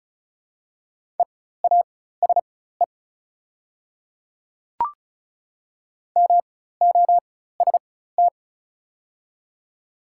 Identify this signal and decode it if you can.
Morse code, heard as a keyed tone, about 36 wpm, spaced as for 16 wpm, 700 Hz. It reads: EASE MOST
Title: 4x Character Spacing - Top 300 Words - 35wpm